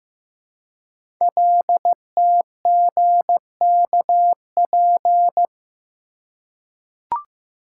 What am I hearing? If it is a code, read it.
Morse code, heard as a keyed tone, about 15 wpm, 700 Hz. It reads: LTGKP
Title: Code Groups - 5 Letter:Numbers - 15wpm